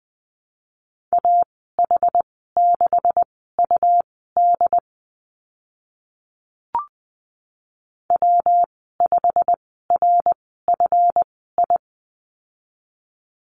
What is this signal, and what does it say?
Morse code, heard as a keyed tone, about 20 wpm, spaced as for 15 wpm, 700 Hz. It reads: AH6UD W5RFI
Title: 140 Lesson 23g - Letter f - Call Signs 20wpm